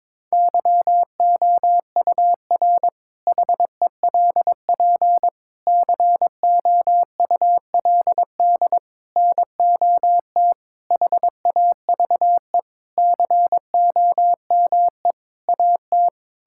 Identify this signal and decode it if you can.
Morse code, heard as a keyed tone, about 22 wpm, 700 Hz. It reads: YOUR HELP COULD NOT HAVE COME AT
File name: practice / 22wpm / Sentences with 8-Word Phrases - 22wpm